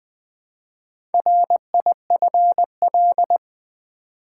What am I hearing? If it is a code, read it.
Morse code, heard as a keyed tone, about 20 wpm, 700 Hz. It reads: RIFL
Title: Build A Word - 1 Letter At A Time - 5-Letter Words - 20wpm